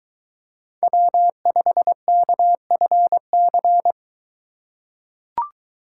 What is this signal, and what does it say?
Morse code, heard as a keyed tone, about 21 wpm, 700 Hz. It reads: W5KFC